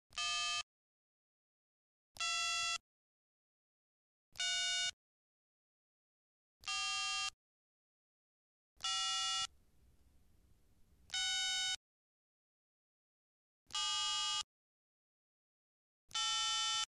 DTMF, heard as telephone touch tones, tones of about 620 ms. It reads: 2 3 4 5 6 7 8